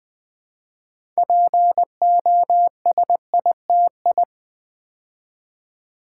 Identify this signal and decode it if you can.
Morse code, heard as a keyed tone, about 20 wpm, 700 Hz. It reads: POSITI
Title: Build A Word - 3 Letters At A Time - 8-Letter Words - 20wpm